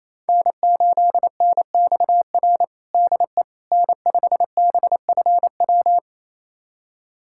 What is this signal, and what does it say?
Morse code, heard as a keyed tone, about 28 wpm, 700 Hz. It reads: N8NXR DE N5BFW
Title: USA Callsign DE Callsign - 5 Characters - 28wpm